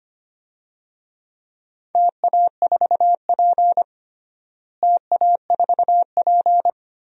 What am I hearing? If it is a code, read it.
Morse code, heard as a keyed tone, about 25 wpm, 700 Hz. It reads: TA4P TA4P
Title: DXCC-100 Call Signs - 4 Characters - Repeated 3x - 25wpm